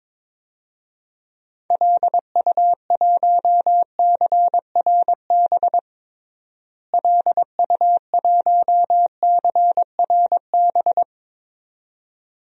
Morse code, heard as a keyed tone, about 22 wpm, 700 Hz. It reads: LU1CRB LU1CRB
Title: DXCC-100 Call Signs - 6 Characters - Repeated 2x - 22wpm